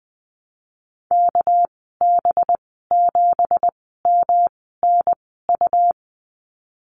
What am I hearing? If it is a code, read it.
Morse code, heard as a keyed tone, about 19 wpm, spaced as for 14 wpm, 700 Hz. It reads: KB7MNU
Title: 207 Lesson 33g - Letter j - Call Signs 20wpm